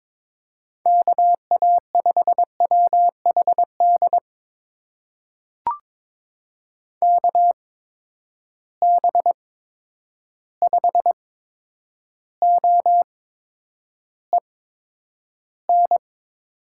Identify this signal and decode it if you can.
Morse code, heard as a keyed tone, about 22 wpm, spaced as for 9 wpm, 700 Hz. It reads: KA5WHD KB5OEN
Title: USA Call Signs - 6 Characters - 8x Character Spacing - 22wpm